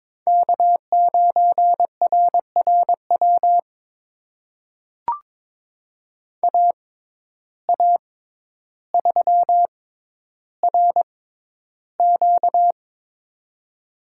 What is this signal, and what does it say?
Morse code, heard as a keyed tone, about 22 wpm, spaced as for 11 wpm, 700 Hz. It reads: K9RRW AA3RQ
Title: USA Call Signs - 5 Characters - 6x Character Spacing - 22wpm